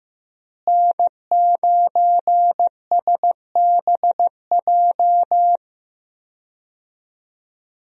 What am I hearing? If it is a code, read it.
Morse code, heard as a keyed tone, about 15 wpm, 700 Hz. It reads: N9SBJ
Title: USA Call Signs - 5 Characters - Repeated 3x - 15wpm